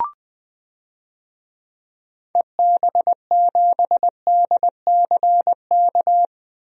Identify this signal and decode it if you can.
Morse code, heard as a keyed tone, about 20 wpm, 700 Hz. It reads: EB7DCK
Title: DXCC-100 Call Signs - 6 Characters - 20wpm